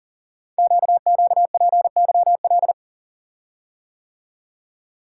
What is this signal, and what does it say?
Morse code, heard as a keyed tone, about 39 wpm, 700 Hz. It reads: QQPYL
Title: Code Groups - 5 Letters - 40wpm